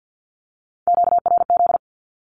Morse code, heard as a keyed tone, about 53 wpm, 700 Hz. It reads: QRZ
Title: Q-Codes 50wpm